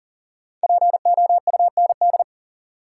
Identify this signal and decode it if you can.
Morse code, heard as a keyed tone, about 37 wpm, 700 Hz. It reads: POUND
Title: Sentences from Top 1000 Words - Review Part 2 - New Word Drill - 40wpm